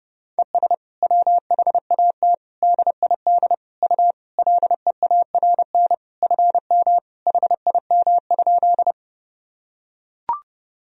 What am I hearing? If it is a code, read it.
Morse code, heard as a keyed tone, about 30 wpm, 700 Hz. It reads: ES WHAT DID U LEARN FM HIM?
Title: Sentences from Top 300 Words and Abbreviations 30wpm